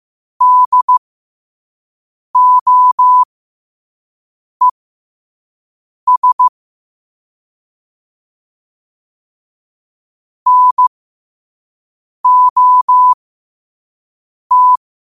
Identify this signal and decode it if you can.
Morse code, heard as a keyed tone, about 15 wpm, spaced as for 5 wpm, 1000 Hz. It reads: DOES NOT